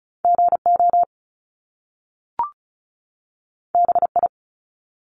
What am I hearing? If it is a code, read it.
Morse code, heard as a keyed tone, about 36 wpm, 700 Hz. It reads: GO BI
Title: Top Two Letter Combinations in Words 35wpm